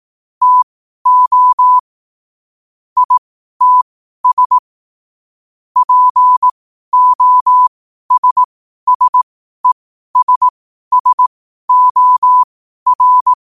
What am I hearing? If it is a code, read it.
Morse code, heard as a keyed tone, about 18 wpm, spaced as for 12 wpm, 1000 Hz. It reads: TO ITS POSSESSOR
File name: qotdfiles / 12wpm